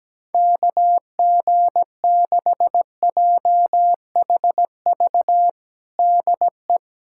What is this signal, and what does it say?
Morse code, heard as a keyed tone, about 17 wpm, 700 Hz. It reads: KG6JHV DE
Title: USA Callsign DE Callsign - 6 Characters - 17wpm